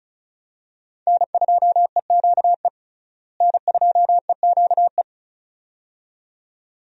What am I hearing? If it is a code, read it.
Morse code, heard as a keyed tone, about 36 wpm, 700 Hz. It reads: N2EQE N2EQE
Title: USA Call Signs - 5 Characters - Repeated 2x - 35wpm